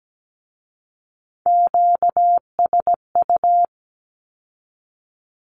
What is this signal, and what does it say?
Morse code, heard as a keyed tone, about 17 wpm, 700 Hz. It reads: QSU
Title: Q-Codes with Spoken Meaning - 17wpm